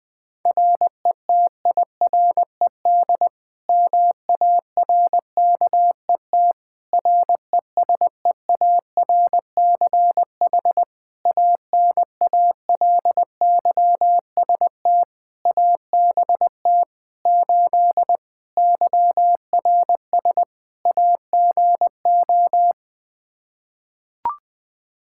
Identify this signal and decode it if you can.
Morse code, heard as a keyed tone, about 20 wpm, 700 Hz. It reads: RETIRED MARKET RESEARCH ANALYST ABT 8 YRS AGO
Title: Practice QSO - Exchange 3 - More Info - 20wpm